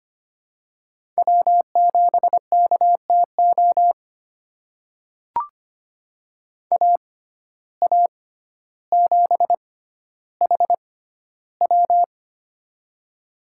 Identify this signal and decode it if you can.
Morse code, heard as a keyed tone, about 25 wpm, spaced as for 13 wpm, 700 Hz. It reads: W7KTO AA7HW
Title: USA Call Signs - 6x Character Spacing - 25wpm